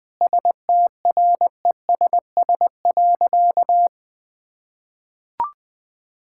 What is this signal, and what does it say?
Morse code, heard as a keyed tone, about 21 wpm, 700 Hz. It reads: STRESS.
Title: Sentences from Top 1000 Words - Review Part 6 - 20wpm